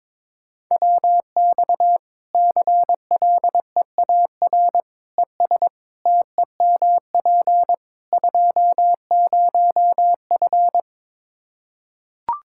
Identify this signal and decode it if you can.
Morse code, heard as a keyed tone, about 22 wpm, 700 Hz. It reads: WX CLEAR ES TEMP 20F
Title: K1USN SST QSO Practice 22wpm